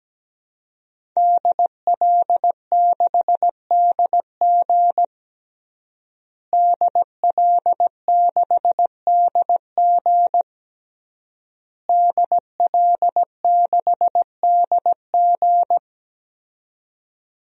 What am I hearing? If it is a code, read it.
Morse code, heard as a keyed tone, about 17 wpm, 700 Hz. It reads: DL6DG DL6DG DL6DG
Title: DXCC-100 Call Signs - 5 Characters - Repeated 3x - 17wpm